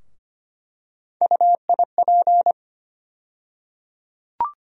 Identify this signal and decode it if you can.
Morse code, heard as a keyed tone, about 25 wpm, 700 Hz. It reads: UIP
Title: USA Call Sign Postfixes - 3 Letters 25wpm